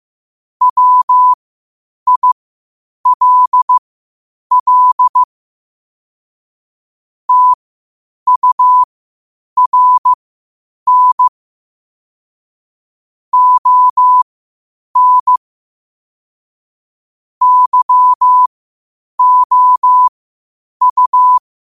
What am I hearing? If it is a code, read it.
Morse code, heard as a keyed tone, about 15 wpm, spaced as for 8 wpm, 1000 Hz. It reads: WILL TURN ON YOU